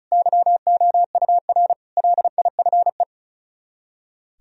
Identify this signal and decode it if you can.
Morse code, heard as a keyed tone, about 44 wpm, 700 Hz. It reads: YOUR LIFE